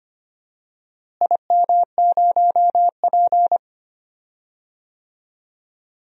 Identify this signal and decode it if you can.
Morse code, heard as a keyed tone, about 25 wpm, 700 Hz. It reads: IM0P
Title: DXCC-100 Call Signs - 4 Characters - Repeated 3x - 25wpm